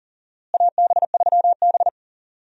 Morse code, heard as a keyed tone, about 40 wpm, 700 Hz. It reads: AB3B
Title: USA Call Signs - 4 Characters - Repeated 3x - 40wpm